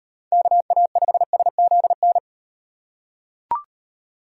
Morse code, heard as a keyed tone, about 35 wpm, 700 Hz. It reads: KA5SZN